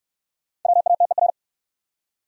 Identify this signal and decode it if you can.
Morse code, heard as a keyed tone, about 67 wpm, 700 Hz. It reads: WATER